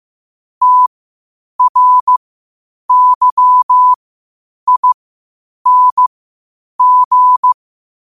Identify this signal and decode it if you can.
Morse code, heard as a keyed tone, about 15 wpm, spaced as for 8 wpm, 1000 Hz. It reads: TRYING
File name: qotdfiles / 8wpm